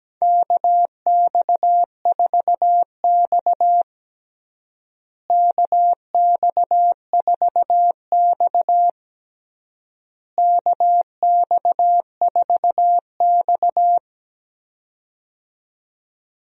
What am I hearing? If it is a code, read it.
Morse code, heard as a keyed tone, about 17 wpm, 700 Hz. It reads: KX4X KX4X KX4X
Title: USA Call Signs - 4 Characters - Repeated 3x - 17wpm